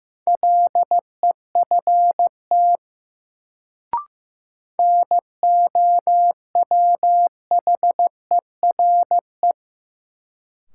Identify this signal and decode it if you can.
Morse code, heard as a keyed tone, about 16 wpm, 700 Hz. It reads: LEFT NOWHERE
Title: Sets of 5 Words Encore - New Word Drill - 15wpm